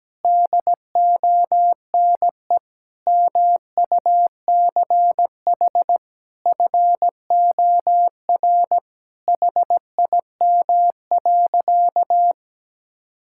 Morse code, heard as a keyed tone, about 17 wpm, 700 Hz. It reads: DONE MUCH FOR HIM.
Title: Sentences from Top 1600 Words 17wpm